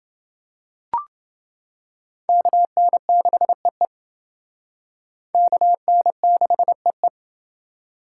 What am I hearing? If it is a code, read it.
Morse code, heard as a keyed tone, about 28 wpm, 700 Hz. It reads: KN6EE KN6EE